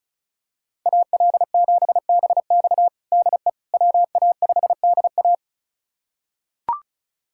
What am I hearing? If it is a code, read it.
Morse code, heard as a keyed tone, about 36 wpm, 700 Hz. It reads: AL7BX DE WA5DA
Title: USA Callsign DE Callsign - 5 Characters - 35wpm